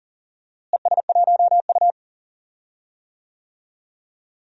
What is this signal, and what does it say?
Morse code, heard as a keyed tone, about 40 wpm, 700 Hz. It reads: ES1U
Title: DXCC-100 Call Signs - 4 Characters - Repeated 2x - 40wpm